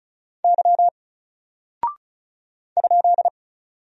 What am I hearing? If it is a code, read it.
Morse code, heard as a keyed tone, about 35 wpm, 700 Hz. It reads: Y ?
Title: Single Character (Letter:Number:Punctuation:Prosign) Alt 35wpm